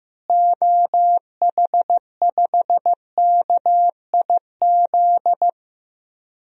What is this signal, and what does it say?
Morse code, heard as a keyed tone, about 15 wpm, 700 Hz. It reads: OH5KIZ
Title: DXCC-100 Call Signs - 6 Characters - Repeated 3x - 15wpm